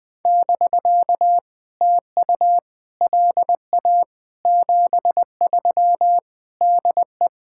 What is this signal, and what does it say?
Morse code, heard as a keyed tone, about 20 wpm, 700 Hz. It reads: <BK> TU LA 73 DE
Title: POTA QSO Practice - 20wpm